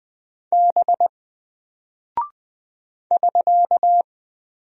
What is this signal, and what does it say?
Morse code, heard as a keyed tone, about 20 wpm, 700 Hz. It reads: B <SK>
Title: Single Character (Letter:Number:Punctuation:Prosign) Alt 20wpm